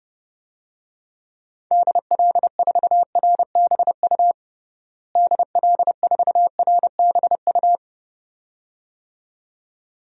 Morse code, heard as a keyed tone, about 31 wpm, 700 Hz. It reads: DL4RBU DL4RBU
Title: DXCC-100 Call Signs - 6 Characters - Repeated 2x - 30wpm